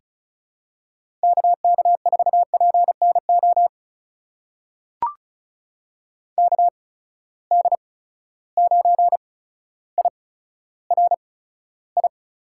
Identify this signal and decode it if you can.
Morse code, heard as a keyed tone, about 36 wpm, spaced as for 15 wpm, 700 Hz. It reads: KK4PNO KD9IRI
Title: USA Call Signs - 6 Characters - 8x Character Spacing - 35wpm